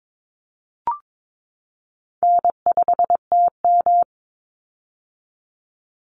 Morse code, heard as a keyed tone, about 22 wpm, 700 Hz. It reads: N5TM
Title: Easy Call Signs - 22wpm